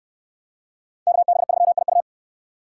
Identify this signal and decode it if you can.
Morse code, heard as a keyed tone, about 64 wpm, 700 Hz. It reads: KB3IU